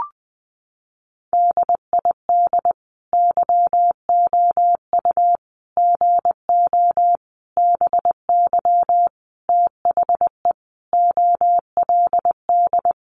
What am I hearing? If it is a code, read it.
Morse code, heard as a keyed tone, about 20 wpm, 700 Hz. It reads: DID YOU GO BY THE OLD
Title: Sentences from Top 1200 Words 20wpm